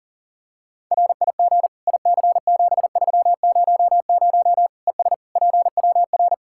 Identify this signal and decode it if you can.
Morse code, heard as a keyed tone, about 40 wpm, 700 Hz. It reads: RIG IC7300 ES PWR
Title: Practice QSO - Exchange 2 - Info - 40wpm